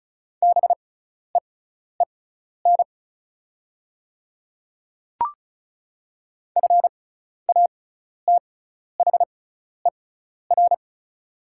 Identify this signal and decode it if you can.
Morse code, heard as a keyed tone, about 36 wpm, spaced as for 12 wpm, 690 Hz. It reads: BEEN FATHER
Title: 6x Character Spacing - Top 200 Words - 35wpm